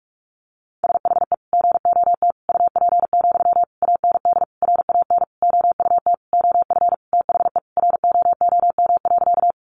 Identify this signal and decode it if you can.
Morse code, heard as a keyed tone, about 44 wpm, 700 Hz. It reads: SHE GOT UP, AND RAN OUT OF THE ROOM.
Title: Sentences from Top 500 Words 45wpm